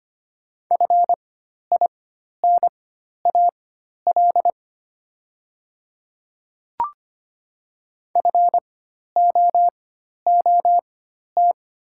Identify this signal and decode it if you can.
Morse code, heard as a keyed tone, about 25 wpm, spaced as for 12 wpm, 700 Hz. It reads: FINAL FOOT
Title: 4x Character Spacing - Top 500 Words - 25wpm